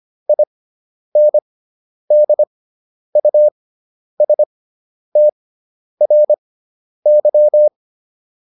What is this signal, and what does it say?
Morse code, heard as a keyed tone, about 25 wpm, spaced as for 10 wpm, 600 Hz. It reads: INDUSTRY